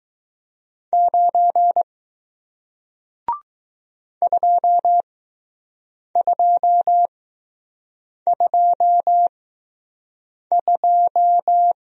Morse code, heard as a keyed tone, about 20 wpm, spaced as for 6 wpm, 700 Hz. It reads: 9 2222